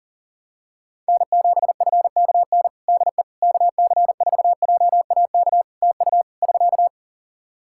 Morse code, heard as a keyed tone, about 40 wpm, 700 Hz. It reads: N7FKN DE KC4JAK TU <SK>
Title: Practice QSO - Exchange 4 - Ending - 40wpm